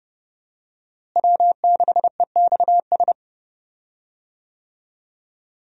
Morse code, heard as a keyed tone, about 30 wpm, 700 Hz. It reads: W6EXS
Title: USA Call Signs - 5 Characters - Repeated 3x - 30wpm